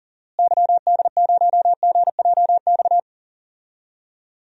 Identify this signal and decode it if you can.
Morse code, heard as a keyed tone, about 40 wpm, 700 Hz. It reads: YD0GJX
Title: DXCC-100 Call Signs - 6 Characters - Repeated 2x - 40wpm